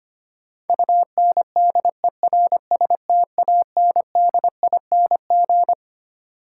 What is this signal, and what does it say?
Morse code, heard as a keyed tone, about 25 wpm, 700 Hz. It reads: UNDERSTANDING
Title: Sentences from Top 1000 Words - Review Part 3 - New Word Drill - 25wpm